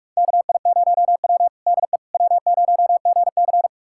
Morse code, heard as a keyed tone, about 46 wpm, 700 Hz. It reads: KI0W DE W0GC
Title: USA Callsign DE Callsign - 4 Characters - 45wpm